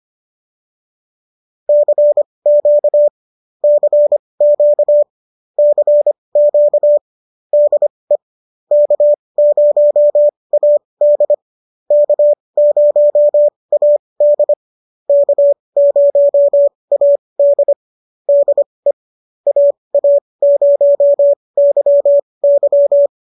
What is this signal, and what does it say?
Morse code, heard as a keyed tone, about 25 wpm, spaced as for 20 wpm, 590 Hz. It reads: CQ CQ CQ DE K0AD K0AD K0AD DE AA0YY